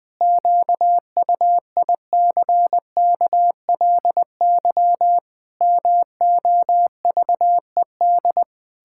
Morse code, heard as a keyed tone, about 20 wpm, 700 Hz. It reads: QUICKLY MOVED
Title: Pangram Challenge v2 - 20wpm